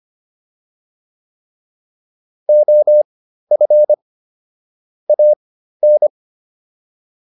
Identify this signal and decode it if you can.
Morse code, heard as a keyed tone, about 25 wpm, spaced as for 13 wpm, 600 Hz. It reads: OF AN